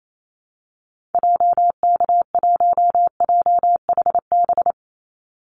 Morse code, heard as a keyed tone, about 28 wpm, 700 Hz. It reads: JK1JHB
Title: Call Signs Repeated 2x - 28wpm